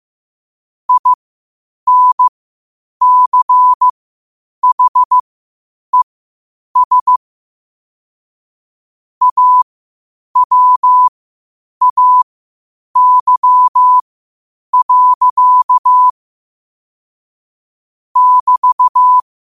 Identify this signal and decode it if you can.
Morse code, heard as a keyed tone, about 15 wpm, spaced as for 8 wpm, 1000 Hz. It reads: INCHES AWAY. <BT>
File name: qotdfiles / 8wpm